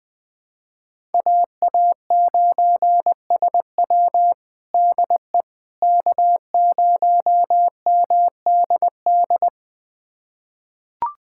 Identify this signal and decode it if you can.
Morse code, heard as a keyed tone, about 20 wpm, 700 Hz. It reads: AA9SW DE K0MDD
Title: USA Callsign DE Callsign 20wpm